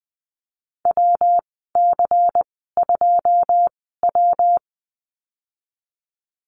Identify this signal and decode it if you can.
Morse code, heard as a keyed tone, about 20 wpm, spaced as for 14 wpm, 700 Hz. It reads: WC2W
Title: 128 Lesson 21g - Number 6 - Call Signs 20wpm